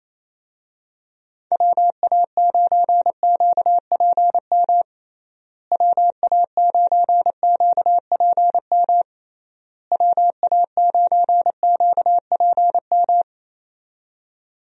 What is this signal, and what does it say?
Morse code, heard as a keyed tone, about 27 wpm, 700 Hz. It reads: WA9QPM WA9QPM WA9QPM
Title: USA Call Signs - 6 Characters - Repeated 3x - 28wpm